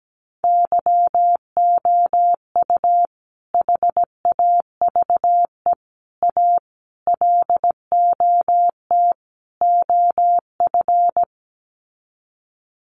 Morse code, heard as a keyed tone, about 17 wpm, 700 Hz. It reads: YOU HAVE A LOT OF
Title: Sets of 5 Words Encore 17wpm